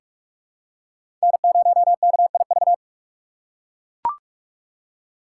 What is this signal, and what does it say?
Morse code, heard as a keyed tone, about 44 wpm, 700 Hz. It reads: N0KIV
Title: USA Call Signs - 5 Characters - 3x Character Spacing - 45wpm